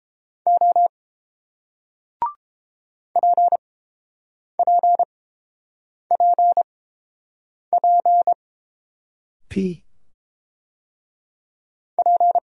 Morse code, heard as a keyed tone, about 29 wpm, spaced as for 7 wpm, 700 Hz. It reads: O PPPP P